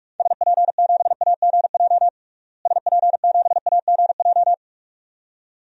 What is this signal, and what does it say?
Morse code, heard as a keyed tone, about 45 wpm, 700 Hz. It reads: SP7AGJ SP7AGJ
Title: DXCC-100 Call Signs - 6 Characters - Repeated 2x - 45wpm